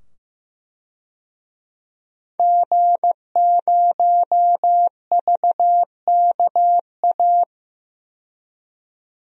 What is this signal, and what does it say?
Morse code, heard as a keyed tone, about 15 wpm, 700 Hz. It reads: G0VKA
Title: DXCC-100 Call Signs - 5 Characters - 15wpm